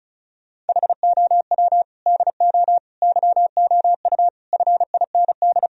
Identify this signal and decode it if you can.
Morse code, heard as a keyed tone, about 36 wpm, 700 Hz. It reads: HOW DO YOU FIND
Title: Sentences from Top 1000 Words - Review Part 4 - 35wpm